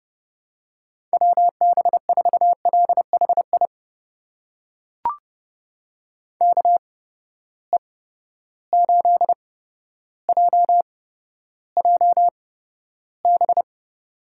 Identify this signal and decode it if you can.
Morse code, heard as a keyed tone, about 30 wpm, spaced as for 13 wpm, 700 Hz. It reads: WB4LHI KE8JJB
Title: USA Call Signs - 6 Characters - 8x Character Spacing - 30wpm